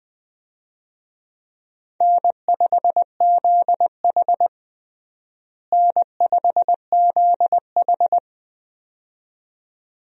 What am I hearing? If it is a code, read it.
Morse code, heard as a keyed tone, about 20 wpm, 700 Hz. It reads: N5ZH N5ZH
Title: USA Call Signs - 4 Characters - Repeated 2x - 20wpm